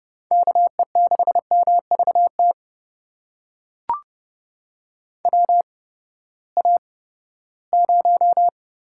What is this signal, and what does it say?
Morse code, heard as a keyed tone, about 30 wpm, spaced as for 17 wpm, 700 Hz. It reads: KE6MVT WA0
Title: USA Call Signs - 6 Characters - 8x Character Spacing - 30wpm